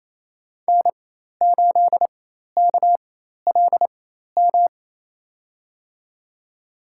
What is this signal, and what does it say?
Morse code, heard as a keyed tone, about 27 wpm, spaced as for 13 wpm, 700 Hz. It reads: N8KLM
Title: USA Call Signs - 5 Characters - 4x Character Spacing - 28wpm